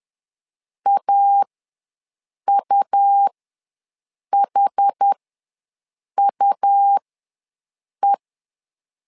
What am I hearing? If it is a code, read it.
Morse code, heard as a keyed tone, about 10 wpm, spaced as for 6 wpm, 800 Hz. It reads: AUHUE
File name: LEZIONI / 2 UHDN / 2 UH ws 10wpm 2029